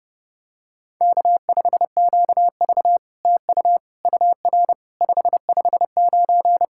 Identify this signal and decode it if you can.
Morse code, heard as a keyed tone, about 30 wpm, 700 Hz. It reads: K5QV TU UR 559
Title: SOTA QSO Practice 30wpm